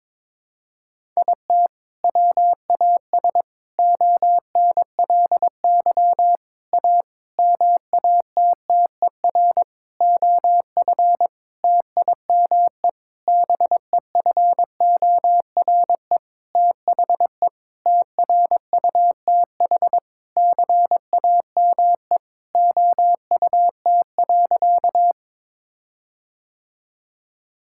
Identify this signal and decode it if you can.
Morse code, heard as a keyed tone, about 22 wpm, 700 Hz. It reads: IT WAS ONLY A MATTER OF TIME BEFORE THE TRUTH CAME OUT.
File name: Sentences with 9-Word Phrases - 22wpm